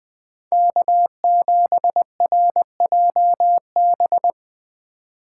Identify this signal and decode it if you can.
Morse code, heard as a keyed tone, about 22 wpm, 700 Hz. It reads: K7RJB
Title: Speed Racing - USA 1x3 Call Signs 20wpm